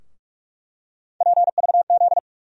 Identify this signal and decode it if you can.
Morse code, heard as a keyed tone, about 41 wpm, 700 Hz. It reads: PVZ
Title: USA Call Sign Postfixes - 3 Letters 45wpm